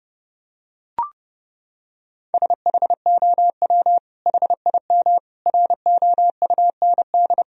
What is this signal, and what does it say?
Morse code, heard as a keyed tone, about 32 wpm, 700 Hz. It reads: SHOW HIM ROUND
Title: Sentences from Top 1000 Words - Review Part 1 - 30wpm